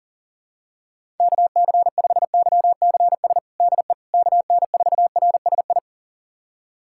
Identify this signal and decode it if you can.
Morse code, heard as a keyed tone, about 40 wpm, 700 Hz. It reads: KC5YCS DE KN4RSI
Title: USA Callsign DE Callsign - 6 Characters - 40wpm